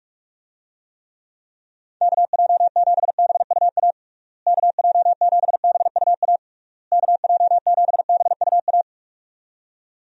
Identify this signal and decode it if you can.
Morse code, heard as a keyed tone, about 47 wpm, 700 Hz. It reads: KJ7BUA KJ7BUA KJ7BUA
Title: USA Call Signs - 6 Characters - Repeated 3x - 45wpm